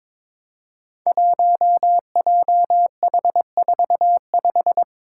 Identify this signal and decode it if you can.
Morse code, heard as a keyed tone, about 22 wpm, 700 Hz. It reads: 1JH45
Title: Five Challenging Letter-Number Combinations 22wpm